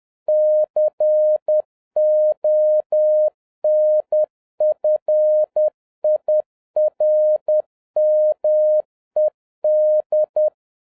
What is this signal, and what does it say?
Morse code, heard as a keyed tone, about 10 wpm, 610 Hz. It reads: CONFIRMED